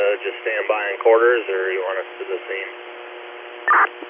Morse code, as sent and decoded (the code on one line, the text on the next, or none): none